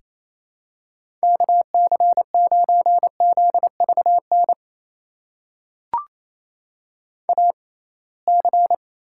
Morse code, heard as a keyed tone, about 28 wpm, spaced as for 21 wpm, 700 Hz. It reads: KC9ZVN AC
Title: USA Call Signs - 6x Character Spacing - 28wpm